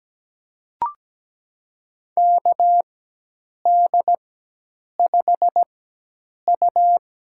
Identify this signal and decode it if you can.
Morse code, heard as a keyed tone, about 17 wpm, spaced as for 8 wpm, 700 Hz. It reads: KD5U
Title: USA Call Signs - 4x Character Spacing - 17wpm